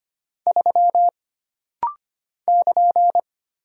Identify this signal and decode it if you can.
Morse code, heard as a keyed tone, about 25 wpm, 700 Hz. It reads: 3 <KN>
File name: Single Character (Letter:Number:Punctuation:Prosign) - Rapid-Fire - 25wpm